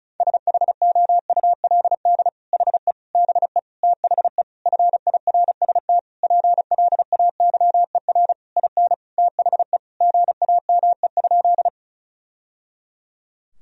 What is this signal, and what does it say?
Morse code, heard as a keyed tone, about 34 wpm, 700 Hz. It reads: SHOULD HE BE THE FIRST PLAYER IN THE GAME?
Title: Sentences from Top 1000 Words - Review Part 4 - 35wpm